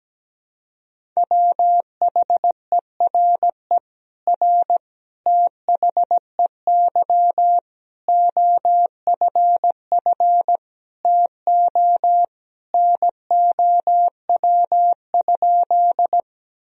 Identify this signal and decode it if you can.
Morse code, heard as a keyed tone, about 17 wpm, 700 Hz. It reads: WHERE R THEY OFF TO NOW?
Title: Sentences from Top 200 Words and Abbreviations 17wpm